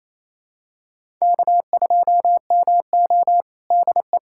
Morse code, heard as a keyed tone, about 28 wpm, 700 Hz. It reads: K2MO DE
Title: USA Callsign DE Callsign - 4 Characters - 28wpm